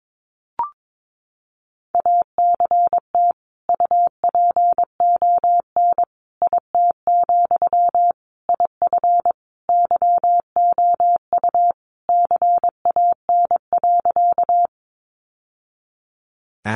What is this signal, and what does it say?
Morse code, heard as a keyed tone, about 22 wpm, 700 Hz. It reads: ACT UPON IT, IF YOU CAN.
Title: Sentences from Top 1100 Words 22wpm